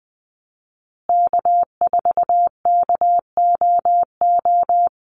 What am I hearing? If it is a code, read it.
Morse code, heard as a keyed tone, about 20 wpm, 700 Hz. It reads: K4KOO
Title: USA Call Signs Repeated 2x 20wpm